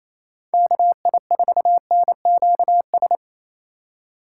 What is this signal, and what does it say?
Morse code, heard as a keyed tone, about 27 wpm, 700 Hz. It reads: KI4NQS
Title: USA Call Signs - 6 Characters - 3x Character Spacing - 28wpm